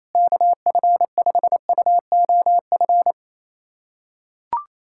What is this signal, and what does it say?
Morse code, heard as a keyed tone, about 28 wpm, 700 Hz. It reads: KF5UOF
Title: USA Callsign DE Callsign - 6 Characters - 28wpm